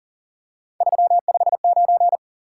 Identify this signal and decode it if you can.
Morse code, heard as a keyed tone, about 40 wpm, 700 Hz. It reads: 359
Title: Top 180 QSO Elements 40wpm